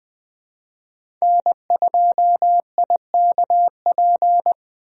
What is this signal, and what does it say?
Morse code, heard as a keyed tone, about 20 wpm, 700 Hz. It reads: N2IKP
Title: USA Call Signs - 5 Characters - Repeated 3x - 20wpm